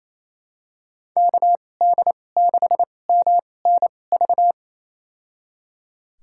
Morse code, heard as a keyed tone, about 28 wpm, spaced as for 20 wpm, 700 Hz. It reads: KD6MNV
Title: USA Call Signs - 6 Characters - 2x Character Spacing - 28wpm